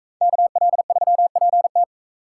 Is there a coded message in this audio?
Yes: Morse code from 0.2 to 1.8 s, heard as a keyed tone, about 39 wpm, 700 Hz. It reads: KL3PT